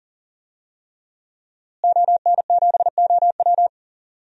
Morse code, heard as a keyed tone, about 39 wpm, 700 Hz. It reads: ON7OW
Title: DXCC-100 Call Signs - 5 Characters - Repeated 3x - 40wpm